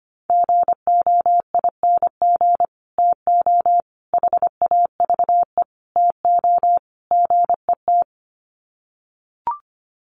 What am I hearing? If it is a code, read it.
Morse code, heard as a keyed tone, about 25 wpm, 700 Hz. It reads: GOING TO HAVE TO GET
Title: Sets of 5 Words Encore2 25wpm